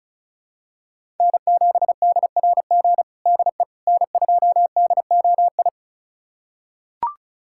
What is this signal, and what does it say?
Morse code, heard as a keyed tone, about 36 wpm, 700 Hz. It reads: N7DRG DE N2DOI
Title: USA Callsign DE Callsign - 5 Characters - 35wpm